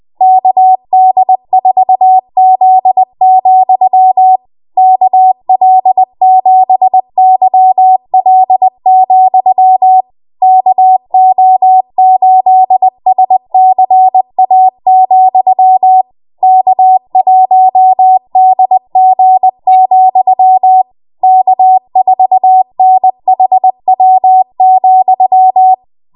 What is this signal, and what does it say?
Morse code, heard as a keyed tone, about 20 wpm, 750 Hz. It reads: KD4Z, KL7YL, KO8SCA, K1DG, K4NHW,
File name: Morse / 20WPM